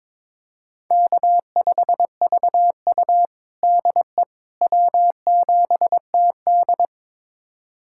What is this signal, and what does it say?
Morse code, heard as a keyed tone, about 21 wpm, 700 Hz. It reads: K5VU DE W7TD
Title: USA Callsign DE Callsign - 4 Characters - 22wpm